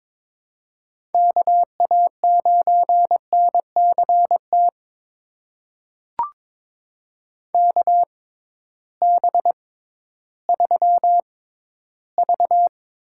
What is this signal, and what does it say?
Morse code, heard as a keyed tone, about 23 wpm, spaced as for 13 wpm, 700 Hz. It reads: KA9NCT KB3V
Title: USA Call Signs - 6 Characters - 6x Character Spacing - 22wpm